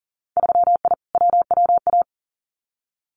Morse code, heard as a keyed tone, about 38 wpm, 700 Hz. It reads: 3I WWA
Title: Winter Field Day Exchanges Encore 40wpm